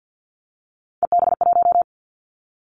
Morse code, heard as a keyed tone, about 50 wpm, 700 Hz. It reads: EB1
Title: Callsign Prefixes for Top 25 DXCC - Repeated 3x - 50wpm